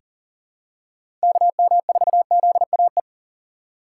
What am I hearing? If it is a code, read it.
Morse code, heard as a keyed tone, about 40 wpm, 700 Hz. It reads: KM4ZAE
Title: USA Call Signs - 3x Character Spacing - 40wpm